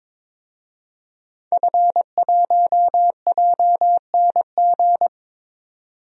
Morse code, heard as a keyed tone, about 22 wpm, 700 Hz. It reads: F1JNG
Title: DXCC-100 Call Signs - 5 Characters - 22wpm